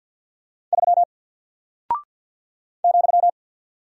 Morse code, heard as a keyed tone, about 48 wpm, 700 Hz. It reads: 3 ,
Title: Single Character (Letter:Number:Punctuation:Prosign) Alt 50wpm